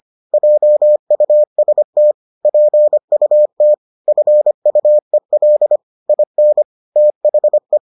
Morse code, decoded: JUST PUT FUEL IN THE

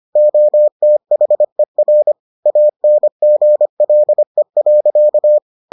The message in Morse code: OTHER ANGLE.

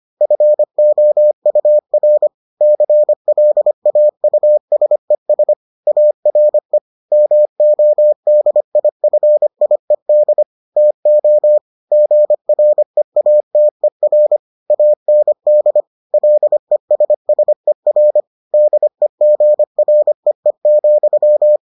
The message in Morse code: FOUR CLAUSES ARE MODIFIED TO GREATER AND LESSER DEGREE,

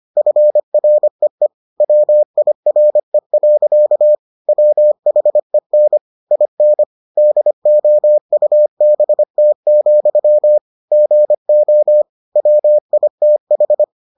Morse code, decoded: FREE WIRE. WHEN IN DOUBT, GO WITH

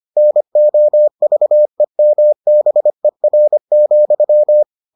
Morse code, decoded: NOVEMBER,

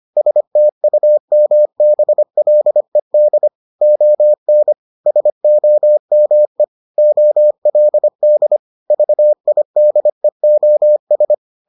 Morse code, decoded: STUMBLED ON SOME OLD VIDEOS

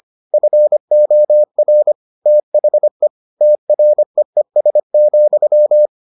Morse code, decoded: FOR THE TREES,

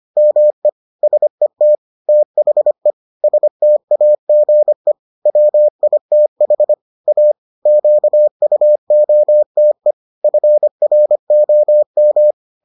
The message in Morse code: ME SET THE STAGE WITH A QUOTE FROM

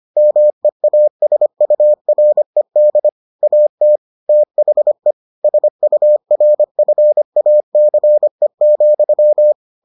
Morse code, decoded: MEASURED AT THE SURFACE,

